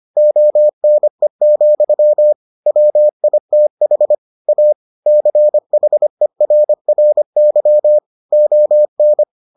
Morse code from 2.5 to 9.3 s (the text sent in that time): WITH A CHERRY ON